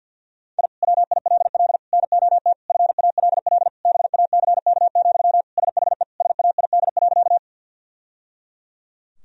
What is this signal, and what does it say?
Morse code, heard as a keyed tone, about 49 wpm, 700 Hz. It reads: I WILL NOT FALL BACK, SHE SAID.
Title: Sentences from Top 1000 Words - Review Part 2 - 50wpm